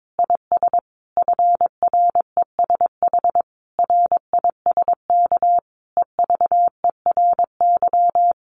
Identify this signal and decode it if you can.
Morse code, heard as a keyed tone, about 22 wpm, 700 Hz. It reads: IS FRESH RISK EVERY
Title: Sentences from Top 1200 Words 22wpm